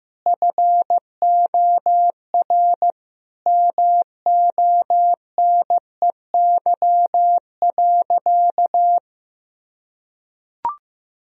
Morse code, heard as a keyed tone, about 16 wpm, 700 Hz. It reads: FOR MONEY.
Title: Sentences from Top 1000 Words - Review Part 4 - 15wpm